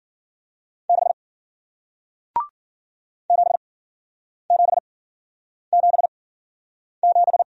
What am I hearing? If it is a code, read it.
Morse code, heard as a keyed tone, about 56 wpm, spaced as for 9 wpm, 700 Hz. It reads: 6 7777